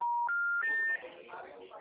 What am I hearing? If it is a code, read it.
Special information tone, with speech.